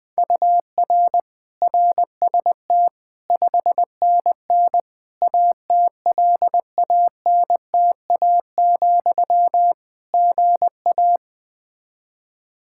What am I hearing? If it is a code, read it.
Morse code, heard as a keyed tone, about 20 wpm, 700 Hz. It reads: UR RST 5NN ATLANTA, GA
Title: Practice QSO - Exchange 1 - Beginning - 20wpm